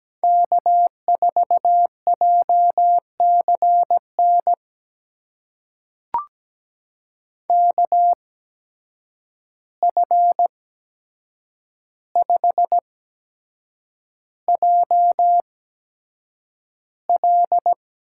Morse code, heard as a keyed tone, about 17 wpm, spaced as for 7 wpm, 700 Hz. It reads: K4JCN KF5JL
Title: USA Call Signs - 5 Characters - 8x Character Spacing - 17wpm